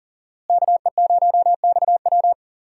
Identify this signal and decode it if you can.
Morse code, heard as a keyed tone, about 40 wpm, 700 Hz. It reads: KE0XW
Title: USA Call Signs - 5 Characters - Repeated 2x - 40wpm